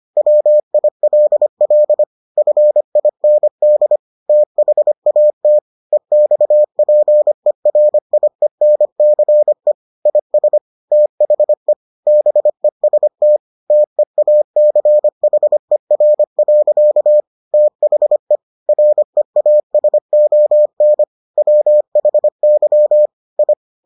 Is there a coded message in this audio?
Yes: Morse code from 0.2 to 23.5 s, heard as a keyed tone, about 25 wpm, 600 Hz. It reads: WILL FIND THAT EXPERIENCE IS THE BEST TEACHER. THE REASON WHY I